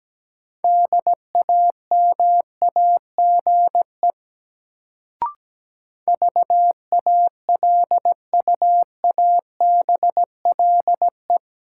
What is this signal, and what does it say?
Morse code, heard as a keyed tone, about 17 wpm, 700 Hz. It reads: DAMAGE VALUABLE
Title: Sentences from Top 1100 Words - New Word Drill Alt - 17wpm